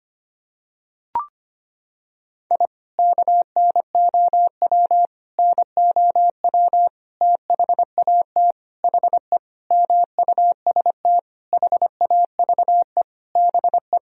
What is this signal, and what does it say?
Morse code, heard as a keyed tone, about 25 wpm, 700 Hz. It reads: I KNOW NOW THAT HE MUST HAVE BE
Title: Sentences from Top 1000 Words - Review Part 1 - 25wpm